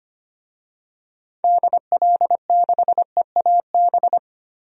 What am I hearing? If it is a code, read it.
Morse code, heard as a keyed tone, about 25 wpm, 700 Hz. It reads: DL6EAB
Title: DXCC-100 Call Signs - 6 Characters - 25wpm